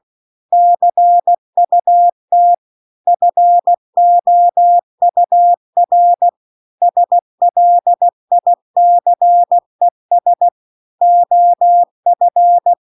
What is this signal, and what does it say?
Morse code, heard as a keyed tone, about 16 wpm, 700 Hz. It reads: CUT FOUR SLICES OF